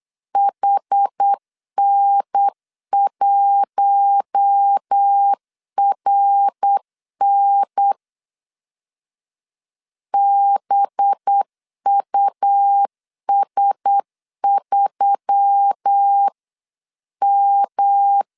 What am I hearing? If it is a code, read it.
Morse code, heard as a keyed tone, about 8 wpm, 800 Hz. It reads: HN1RN BUS3M
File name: CW / DET / 1 6 1500C S B 8wpm 5142